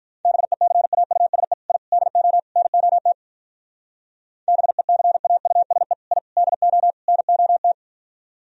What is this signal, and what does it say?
Morse code, heard as a keyed tone, about 44 wpm, 700 Hz. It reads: BECAUSE I DO NOT BECAUSE I DO NOT
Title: Speed-Racing - Sets of 4 Words 35wpm